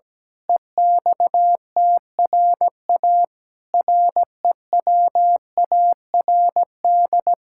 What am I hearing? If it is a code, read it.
Morse code, heard as a keyed tone, about 17 wpm, 700 Hz. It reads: EXTRA REWARD
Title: Pangram Challenge v2 - 17wpm